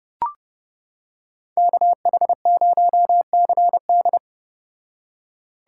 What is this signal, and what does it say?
Morse code, heard as a keyed tone, about 29 wpm, 700 Hz. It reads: KH0CD